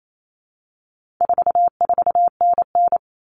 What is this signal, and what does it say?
Morse code, heard as a keyed tone, about 28 wpm, 700 Hz. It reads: 44NN
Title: Common Cut Numbers - 4-Digit 28wpm